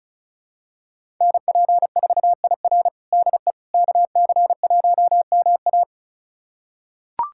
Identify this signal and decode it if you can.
Morse code, heard as a keyed tone, about 36 wpm, 700 Hz. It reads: NP4IR DE KC1MA
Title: USA Callsign DE Callsign - 5 Characters - 35wpm